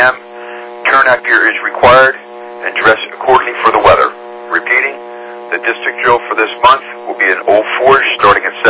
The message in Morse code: none

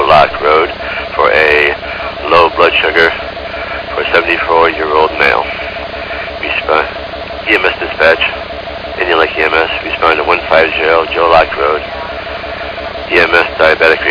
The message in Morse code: none

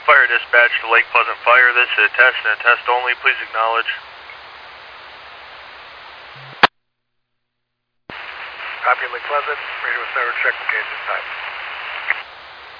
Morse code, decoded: none

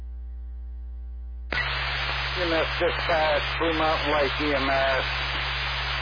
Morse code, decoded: none

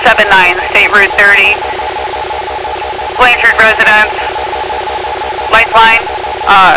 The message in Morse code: none

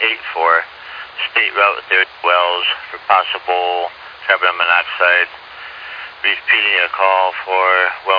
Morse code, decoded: none

